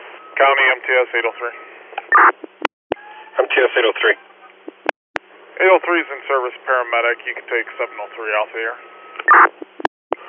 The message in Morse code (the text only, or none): none